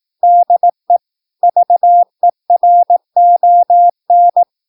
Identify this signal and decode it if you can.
Morse code, heard as a keyed tone, about 18 wpm, 700 Hz. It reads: DE VERON
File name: PI4AA Uitzending